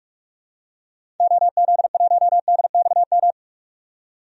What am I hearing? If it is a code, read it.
Morse code, heard as a keyed tone, about 46 wpm, 700 Hz. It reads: OZ1DXM
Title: DXCC-100 Call Signs - 6 Characters - 45wpm